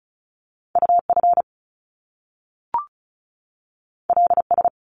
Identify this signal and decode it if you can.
Morse code, heard as a keyed tone, about 36 wpm, 700 Hz. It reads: UF LS
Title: Top Two Letter Combinations in Words 35wpm